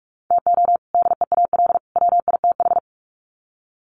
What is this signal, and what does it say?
Morse code, heard as a keyed tone, about 44 wpm, 700 Hz. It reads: TO DEAL WITH
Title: Sets of 3 Words 45wpm